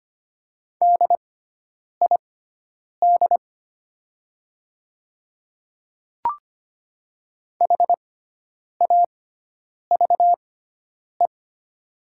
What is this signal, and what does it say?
Morse code, heard as a keyed tone, about 25 wpm, spaced as for 9 wpm, 700 Hz. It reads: DID HAVE
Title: 6x Character Spacing - Top 100 Words - 25wpm